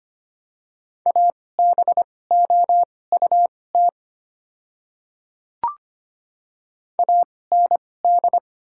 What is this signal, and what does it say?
Morse code, heard as a keyed tone, about 25 wpm, spaced as for 18 wpm, 700 Hz. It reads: ABOUT AND
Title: 2x Character Spacing - Top 100 Words - 25wpm